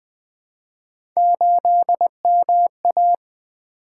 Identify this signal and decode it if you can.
Morse code, heard as a keyed tone, about 20 wpm, 700 Hz. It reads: 8MA
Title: Three Letter-Number Combinations - 20wpm